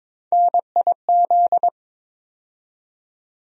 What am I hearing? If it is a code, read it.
Morse code, heard as a keyed tone, about 22 wpm, 700 Hz. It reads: NIZ
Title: USA Call Sign Postfixes - 3 Letters 22wpm